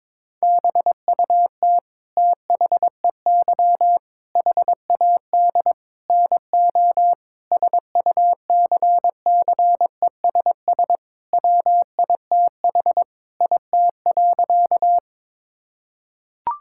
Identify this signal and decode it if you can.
Morse code, heard as a keyed tone, about 22 wpm, 700 Hz. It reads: BUT THEY HAD NO SUCCESS WITH IT.
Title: Sentences from Top 1000 Words - Review Part 6 - 22wpm